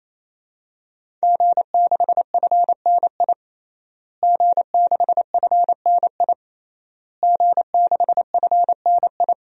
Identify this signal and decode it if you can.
Morse code, heard as a keyed tone, about 28 wpm, 700 Hz. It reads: G6FNI G6FNI G6FNI
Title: DXCC-100 Call Signs - 5 Characters - Repeated 3x - 28wpm